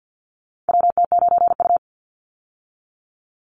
Morse code, heard as a keyed tone, about 50 wpm, 700 Hz. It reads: WT9U
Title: Call Signs Repeated 3x - 50wpm